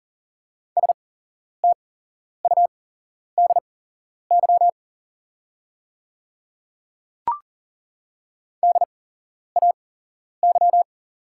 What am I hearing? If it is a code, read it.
Morse code, heard as a keyed tone, about 40 wpm, spaced as for 11 wpm, 690 Hz. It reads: STUDY DAY
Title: 8x Character Spacing - Top 300 Words - 40wpm